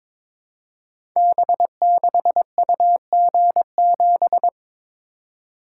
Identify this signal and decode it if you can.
Morse code, heard as a keyed tone, about 22 wpm, 700 Hz. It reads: B6UG7
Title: Code Groups - 5 Letter:Numbers - 22wpm